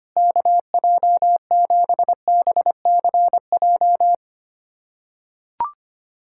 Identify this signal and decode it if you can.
Morse code, heard as a keyed tone, about 25 wpm, 700 Hz. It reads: KJ7BCJ
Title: USA Callsign DE Callsign 25wpm